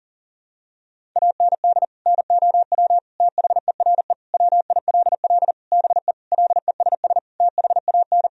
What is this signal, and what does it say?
Morse code, heard as a keyed tone, about 40 wpm, 700 Hz. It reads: AND NOW THERE WILL BE LESS THAN
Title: Sentences from Top 1000 Words - Review Part 2 - 40wpm